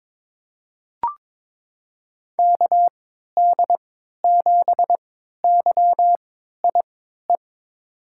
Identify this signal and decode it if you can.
Morse code, heard as a keyed tone, about 22 wpm, spaced as for 13 wpm, 700 Hz. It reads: KD7YIE
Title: USA Call Signs - 3x Character Spacing - 22wpm